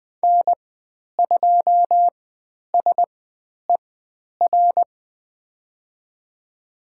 Morse code, heard as a keyed tone, about 20 wpm, spaced as for 10 wpm, 700 Hz. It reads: N2SER
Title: 109 Lesson 18g - Letter m - Call Signs 20wpm